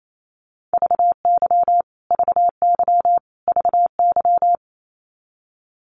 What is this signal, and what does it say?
Morse code, heard as a keyed tone, about 28 wpm, 700 Hz. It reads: VY VY VY
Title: Abbreviations Repeated 3x - 28wpm